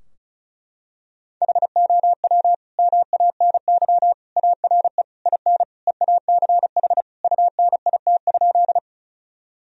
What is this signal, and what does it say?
Morse code, heard as a keyed tone, about 36 wpm, 700 Hz. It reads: HOW MANY ARE IN EACH UNIT?
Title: Sentences from Top 1000 Words - Review Part 2 - 35wpm